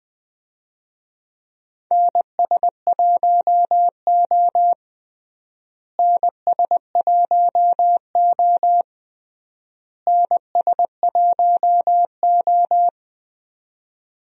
Morse code, heard as a keyed tone, about 21 wpm, 700 Hz. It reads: NS1O NS1O NS1O
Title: USA Call Signs - 4 Characters - Repeated 3x - 20wpm